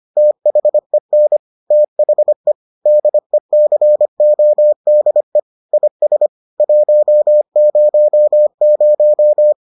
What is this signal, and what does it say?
Morse code, heard as a keyed tone, about 25 wpm, 600 Hz. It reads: THEN THE DECODE IS 100